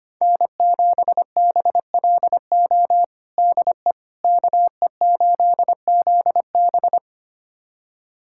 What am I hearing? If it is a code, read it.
Morse code, heard as a keyed tone, about 25 wpm, 700 Hz. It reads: N7BLO DE KE8ZB
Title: USA Callsign DE Callsign - 5 Characters - 25wpm